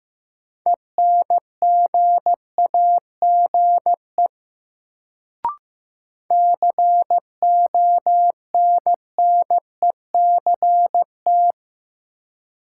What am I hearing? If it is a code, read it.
Morse code, heard as a keyed tone, about 15 wpm, 700 Hz. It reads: ENGAGE CONNECT
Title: Sentences from Top 1500 Words - New Word Drill - 15wpm